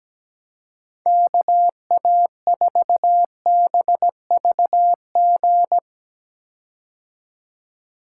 Morse code, heard as a keyed tone, about 17 wpm, 700 Hz. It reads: KA4BVG
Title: K1USN SST QSO Practice - 17wpm